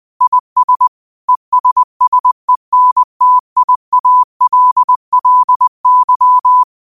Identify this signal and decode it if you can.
Morse code, heard as a keyed tone, about 20 wpm, 1000 Hz. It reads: IS ESSENTIALLY
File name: qotdfiles / 20wpm